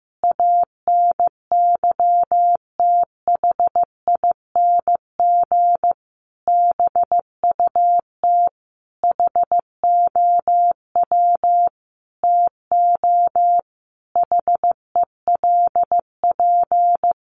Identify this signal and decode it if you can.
Morse code, heard as a keyed tone, about 15 wpm, 700 Hz. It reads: ANYTHING BUT HOW TO HELP
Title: Sentences from Top 1000 Words 15wpm